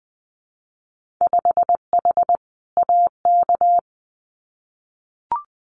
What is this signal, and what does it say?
Morse code, heard as a keyed tone, about 20 wpm, 700 Hz. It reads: 5H AK
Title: Winter Field Day Exchanges Encore 20wpm